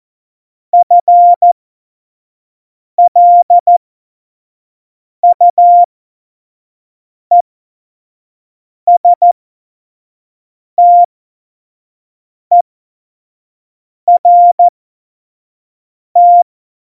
Morse code, heard as a keyed tone, about 14 wpm, spaced as for 5 wpm, 700 Hz. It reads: FLUESTERT
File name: Morsekurs / Witze / Witz 01. Dezember, 2020 25 F70